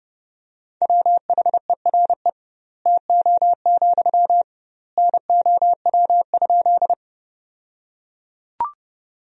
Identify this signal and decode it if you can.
Morse code, heard as a keyed tone, about 30 wpm, 700 Hz. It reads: WHERE TO, NOW?
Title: 2x Word-Spacing - Sentences from Top 200 Words - 30wpm